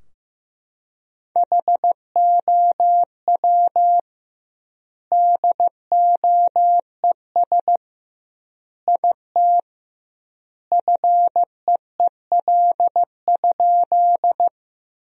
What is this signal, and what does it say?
Morse code, heard as a keyed tone, about 16 wpm, 700 Hz. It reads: HOW DOES IT FEEL?
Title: 2x Word-Spacing - Sentences from Top 400 Words - 15wpm